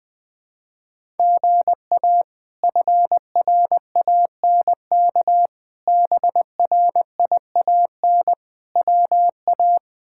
Morse code, heard as a keyed tone, about 21 wpm, 700 Hz. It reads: GA FRANK BRIAN WA
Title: K1USN SST QSO Practice - 20wpm